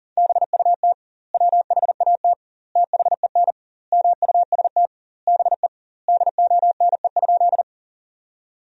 Morse code, heard as a keyed tone, about 40 wpm, 700 Hz. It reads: BUT WHAT THEN MUST BE DONE?
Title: 2x Word-Spacing - Sentences from Top 500 Words - 40wpm